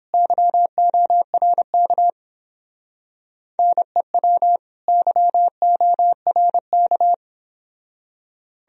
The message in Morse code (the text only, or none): YORK NEW YORK